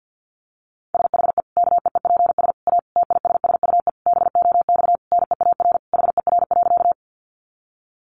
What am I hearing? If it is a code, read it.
Morse code, heard as a keyed tone, about 50 wpm, 700 Hz. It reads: SHE KEEPS A TISSUE BOX NEAR HER.
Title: Easy Phrases 50wpm